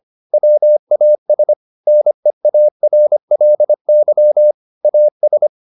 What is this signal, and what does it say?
Morse code, heard as a keyed tone, about 25 wpm, 600 Hz. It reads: WAS NEARLY AS
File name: the old man and the sea part 1 25wpm